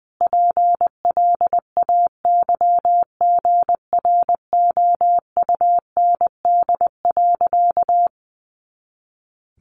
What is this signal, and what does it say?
Morse code, heard as a keyed tone, about 20 wpm, 700 Hz. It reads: PLAYGROUND.
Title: Additional Easy Phrases 20wpm